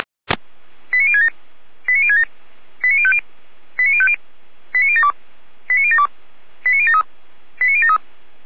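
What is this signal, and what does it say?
Five-tone selective call (ZVEI), heard as quick runs of five tones, tones of about 70 ms. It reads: 89067 89067 89050 89050 89072 89072 89073 89073
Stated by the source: ZVEI five-tone call 89011, received by SDR.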